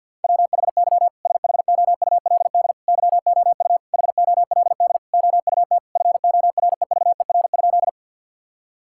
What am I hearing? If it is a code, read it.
Morse code, heard as a keyed tone, about 51 wpm, 700 Hz. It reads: WHY SHOULD YOU HOLD OUT FOREVER?
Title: Sentences from Top 1600 Words 50wpm